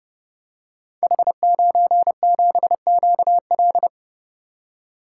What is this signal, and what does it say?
Morse code, heard as a keyed tone, about 30 wpm, 700 Hz. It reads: H97QL
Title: Code Groups - 5 Letter:Numbers - 30wpm